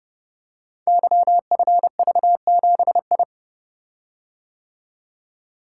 Morse code, heard as a keyed tone, about 29 wpm, 700 Hz. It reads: YFV7I
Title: Code Groups - 5 Letter:Numbers - 30wpm